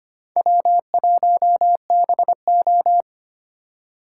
Morse code, heard as a keyed tone, about 32 wpm, 700 Hz. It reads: W1BO